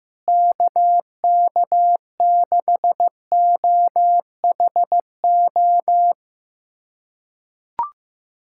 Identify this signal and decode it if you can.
Morse code, heard as a keyed tone, about 15 wpm, 700 Hz. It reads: KK6OHO
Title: USA Callsign DE Callsign 15wpm